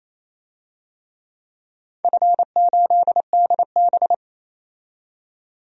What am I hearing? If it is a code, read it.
Morse code, heard as a keyed tone, about 28 wpm, 700 Hz. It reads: F8DB
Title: DXCC-100 Call Signs - 4 Characters - 28wpm